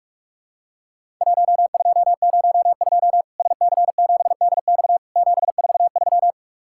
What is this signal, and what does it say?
Morse code, heard as a keyed tone, about 46 wpm, 700 Hz. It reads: 1202 SC7DX 743
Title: Worked All Europe DX Contest - QTC Exchange - 45wpm